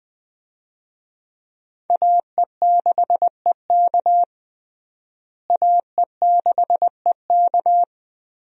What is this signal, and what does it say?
Morse code, heard as a keyed tone, about 20 wpm, 700 Hz. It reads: AE6EK AE6EK
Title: USA Call Signs - 5 Characters - Repeated 3x - 20wpm